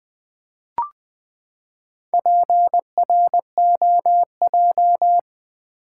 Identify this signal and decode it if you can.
Morse code, heard as a keyed tone, about 20 wpm, 700 Hz. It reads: PROJ
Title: Build A Word - 4 Letters At A Time - 10-Letter Words - 20wpm